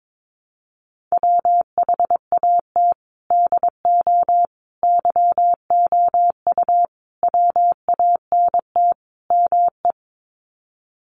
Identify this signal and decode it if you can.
Morse code, heard as a keyed tone, about 22 wpm, 700 Hz. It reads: WHAT DO YOU WANT ME
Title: Sets of 5 Words Encore2 22wpm